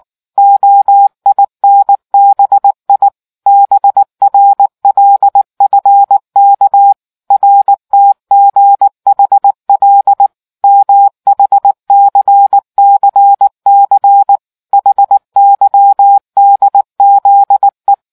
Morse code, heard as a keyed tone, about 19 wpm, 800 Hz. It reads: OINBI BRLFK RTGHL MHCCC HYDZE